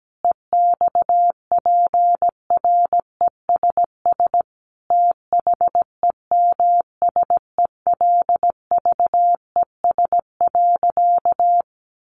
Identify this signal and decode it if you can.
Morse code, heard as a keyed tone, about 17 wpm, 700 Hz. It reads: EXPRESS THEMSELVES.
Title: Sentences from Top 1100 Words 17wpm